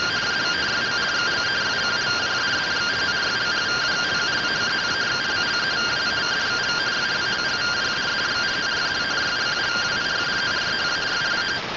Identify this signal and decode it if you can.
Two-tone teleprinter signal, about 170 Hz shift.